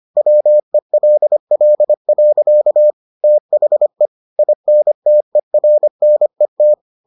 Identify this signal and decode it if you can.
Morse code, heard as a keyed tone, about 25 wpm, 600 Hz. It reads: WELL. THE INTERNET